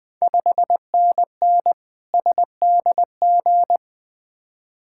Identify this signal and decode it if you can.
Morse code, heard as a keyed tone, about 20 wpm, 700 Hz. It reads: 5NN SDG
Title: Ontario QSO Party Exchange 20wpm